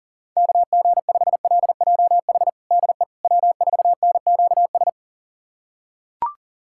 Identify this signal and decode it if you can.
Morse code, heard as a keyed tone, about 40 wpm, 700 Hz. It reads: KG5LJH DE W4NQS
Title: USA Callsign DE Callsign 40wpm